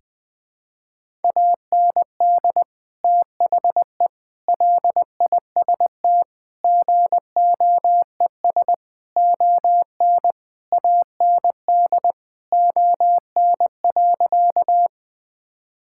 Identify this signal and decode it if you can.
Morse code, heard as a keyed tone, about 20 wpm, 700 Hz. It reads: AND THE LIST GOES ON AND ON.
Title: Sentences with 7-Word Phrases - 20wpm